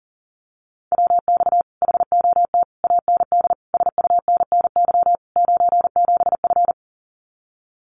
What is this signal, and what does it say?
Morse code, heard as a keyed tone, about 40 wpm, 700 Hz. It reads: WX HOT AND SUNNY 97F
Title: Common QSO Phrases Encore 40wpm